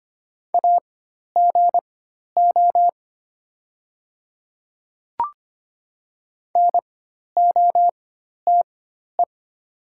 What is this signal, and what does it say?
Morse code, heard as a keyed tone, about 25 wpm, spaced as for 12 wpm, 700 Hz. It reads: AGO NOTE
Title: 4x Character Spacing - Top 500 Words - 25wpm